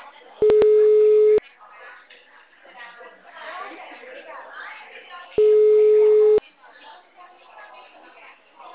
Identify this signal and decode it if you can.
Ringback tone (Europe, 425 Hz), with speech.
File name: out- -2017- - - .96112